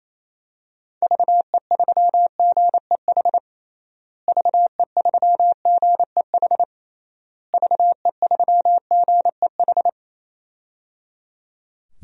Morse code, heard as a keyed tone, about 28 wpm, 700 Hz. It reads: VE3GEH VE3GEH VE3GEH
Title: DXCC-100 Call Signs - 6 Characters - Repeated 3x - 28wpm